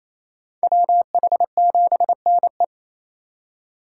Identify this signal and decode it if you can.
Morse code, heard as a keyed tone, about 28 wpm, 700 Hz. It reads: WH7NE
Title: USA Call Signs - 5 Characters - 2x Character Spacing - 28wpm